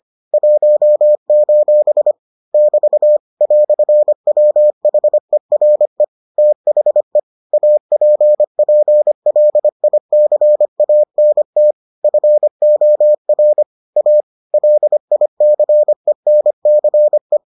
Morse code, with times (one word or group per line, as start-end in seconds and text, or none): 0.3-2.1 1:
2.5-3.2 <BT>
3.4-6.0 "WHERE
6.4-7.2 THE
7.5-11.7 APPLICANT
12.0-13.6 FOR
14.0-14.2 A
14.5-17.4 LICENCE